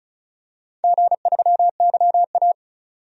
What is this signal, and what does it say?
Morse code, heard as a keyed tone, about 36 wpm, 700 Hz. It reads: G3YA
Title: DXCC-100 Call Signs - 4 Characters - 35wpm